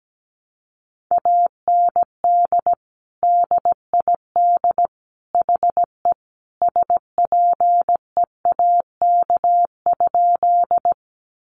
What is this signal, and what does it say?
Morse code, heard as a keyed tone, about 17 wpm, 700 Hz. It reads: AND DID HE SPEAK?
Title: Sentences from Top 1100 Words 17wpm